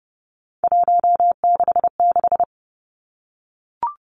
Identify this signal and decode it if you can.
Morse code, heard as a keyed tone, about 32 wpm, 700 Hz. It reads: 166
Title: Three Challenging Letter-Number Combinations 30wpm